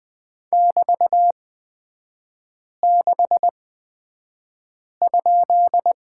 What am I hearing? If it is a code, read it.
Morse code, heard as a keyed tone, about 20 wpm, spaced as for 5 wpm, 700 Hz. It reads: <BT>6?
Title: 224 Lesson 36e - BT - Three Characters 20wpm